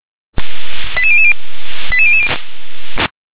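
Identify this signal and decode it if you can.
Five-tone selective call (ZVEI), heard as quick runs of five tones, tones of about 70 ms. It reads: 88000 88000
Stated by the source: Five-tone ZVEI selective call received by SDR.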